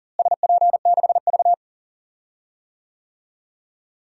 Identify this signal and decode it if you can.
Morse code, heard as a keyed tone, about 37 wpm, 700 Hz. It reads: SP6V
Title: DXCC-100 Call Signs - 4 Characters - Repeated 3x - 40wpm